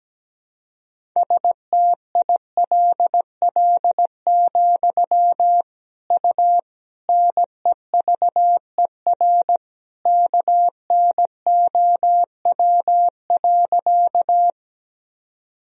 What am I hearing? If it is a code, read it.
Morse code, heard as a keyed tone, about 17 wpm, 700 Hz. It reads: STILL, U NEVER KNOW.
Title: Sentences from Top 300 Words and Abbreviations 17wpm